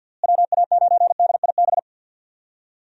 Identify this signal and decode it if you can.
Morse code, heard as a keyed tone, about 51 wpm, 700 Hz. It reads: WA9DIB
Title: USA Call Signs - 6 Characters - Repeated 2x - 50wpm